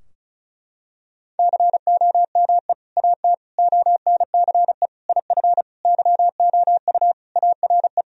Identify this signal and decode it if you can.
Morse code, heard as a keyed tone, about 36 wpm, 700 Hz. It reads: COME AT ONCE IF YOU ARE
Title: Sentences from Top 1000 Words - Review Part 2 - 35wpm